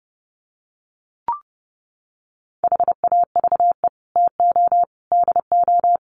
Morse code, heard as a keyed tone, about 30 wpm, 700 Hz. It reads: HAVE TO DO
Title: Sets of 3 Words 30wpm